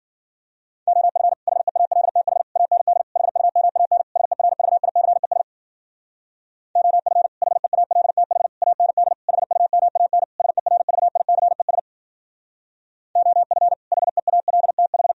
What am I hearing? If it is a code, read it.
Morse code, heard as a keyed tone, about 52 wpm, 700 Hz. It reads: OF HEALTH AND HUMAN SERVICES OF HEALTH AND HUMAN SERVICES OF HEALTH